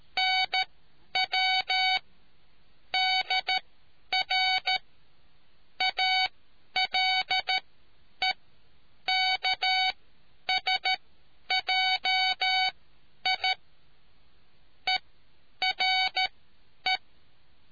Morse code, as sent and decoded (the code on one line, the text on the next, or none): -. .-- / -.. .-. / .- .-.. -.- ... .--- .. / . .-. .
NW DR ALEKSJI ERE